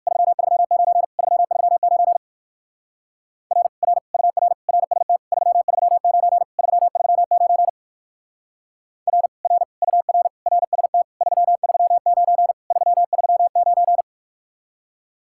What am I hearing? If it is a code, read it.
Morse code, heard as a keyed tone, about 56 wpm, 700 Hz. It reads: 339 339 R R UR RST 339 339 R R UR RST 339 339
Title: Speed-Racing Common QSO Phrases Encore - 45wpm